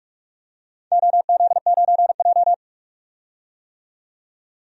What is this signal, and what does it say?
Morse code, heard as a keyed tone, about 42 wpm, 700 Hz. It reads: OZ9J
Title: DXCC-100 Call Signs - 4 Characters - 45wpm